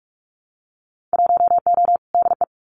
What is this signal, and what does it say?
Morse code, heard as a keyed tone, about 44 wpm, 700 Hz. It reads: 1O DE
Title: Winter Field Day Exchanges 45wpm